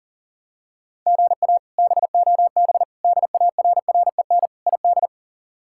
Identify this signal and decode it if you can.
Morse code, heard as a keyed tone, about 37 wpm, 700 Hz. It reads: GA BOB DARREN ID
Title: K1USN SST QSO Practice - 40wpm